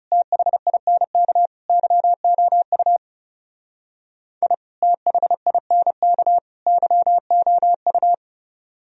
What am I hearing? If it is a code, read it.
Morse code, heard as a keyed tone, about 36 wpm, 700 Hz. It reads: THINK YOU I THINK YOU
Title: Speed-Racing - Sets of 3 Words 30wpm